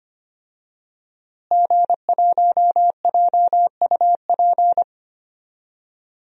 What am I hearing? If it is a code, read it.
Morse code, heard as a keyed tone, about 25 wpm, 700 Hz. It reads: G1JUP
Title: DXCC-100 Call Signs - 5 Characters - Repeated 3x - 25wpm